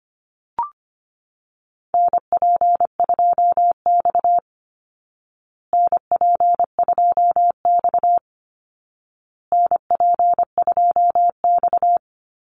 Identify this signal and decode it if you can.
Morse code, heard as a keyed tone, about 25 wpm, 700 Hz. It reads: NP2X NP2X NP2X
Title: Call Signs Repeated 2x - 25wpm